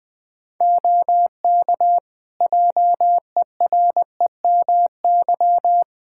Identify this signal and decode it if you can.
Morse code, heard as a keyed tone, about 20 wpm, 700 Hz. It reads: OK JEREMY
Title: Practice QSO - Exchange 2 - Info - 20wpm